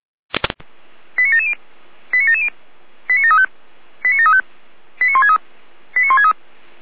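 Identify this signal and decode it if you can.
Five-tone selective call (ZVEI), heard as quick runs of five tones, tones of about 70 ms. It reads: 89770 89770 89735 89735 89163 89163
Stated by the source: ZVEI five-tone call received via SDR.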